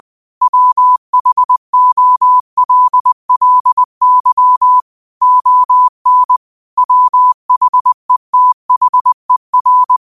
Morse code, heard as a keyed tone, about 20 wpm, 1000 Hz. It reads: WHOLLY ON WHETHER
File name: qotdfiles / 20wpm